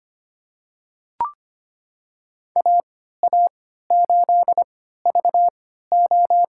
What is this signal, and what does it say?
Morse code, heard as a keyed tone, about 25 wpm, spaced as for 14 wpm, 700 Hz. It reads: AA8VO
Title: USA Call Signs - 5 Characters - 3x Character Spacing - 25wpm